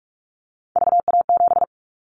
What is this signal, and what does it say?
Morse code, heard as a keyed tone, about 46 wpm, 700 Hz. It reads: VA7
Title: Callsign Prefixes for Top 50 DXCC - 45wpm